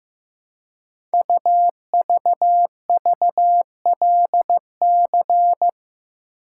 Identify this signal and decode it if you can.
Morse code, heard as a keyed tone, about 15 wpm, 700 Hz. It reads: UVVLC
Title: Code Groups - 5 Letters - 15wpm